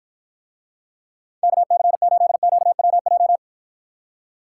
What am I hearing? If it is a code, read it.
Morse code, heard as a keyed tone, about 54 wpm, 700 Hz. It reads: KC8QWJ